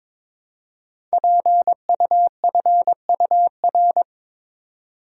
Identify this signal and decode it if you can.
Morse code, heard as a keyed tone, about 22 wpm, 700 Hz. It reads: PUFUR
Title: Code Groups - 5 Letters - 22wpm